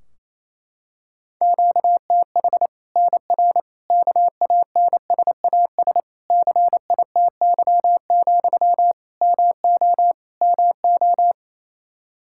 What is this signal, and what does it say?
Morse code, heard as a keyed tone, about 28 wpm, 700 Hz. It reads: QTH NR KANSAS CITY, MO MO
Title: Practice QSO - Exchange 1 - Beginning - 28wpm